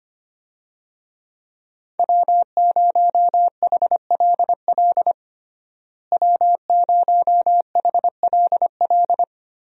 Morse code, heard as a keyed tone, about 25 wpm, 700 Hz. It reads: W0HLL W0HLL
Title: USA Call Signs - 5 Characters - Repeated 2x - 25wpm